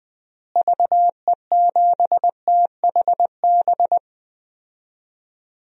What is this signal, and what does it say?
Morse code, heard as a keyed tone, about 20 wpm, 700 Hz. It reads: VE7THB
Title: DXCC-100 Call Signs - 6 Characters - Repeated 2x - 20wpm